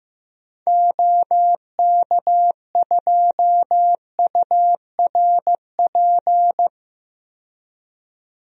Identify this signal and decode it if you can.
Morse code, heard as a keyed tone, about 15 wpm, 700 Hz. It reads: OK2URP
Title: DXCC-100 Call Signs - 6 Characters - 15wpm